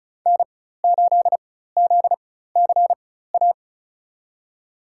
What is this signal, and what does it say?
Morse code, heard as a keyed tone, about 34 wpm, spaced as for 16 wpm, 700 Hz. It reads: N8ZCA
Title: USA Call Signs - 5 Characters - 4x Character Spacing - 35wpm